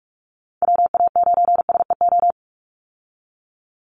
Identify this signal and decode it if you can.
Morse code, heard as a keyed tone, about 44 wpm, 700 Hz. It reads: WA9SEO
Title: Call Signs Repeated 3x - 45wpm